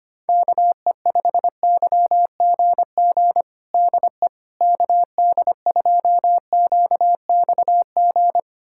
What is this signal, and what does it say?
Morse code, heard as a keyed tone, about 25 wpm, 700 Hz. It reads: KE5YGG DE KD2QXG
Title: USA Callsign DE Callsign - 6 Characters - 25wpm